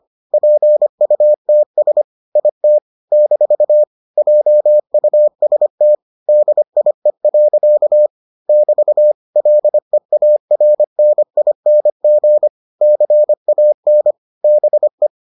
Morse code, heard as a keyed tone, about 25 wpm, 600 Hz. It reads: PUTS IT - JUST DIE. <BT> LEARNING CAN BE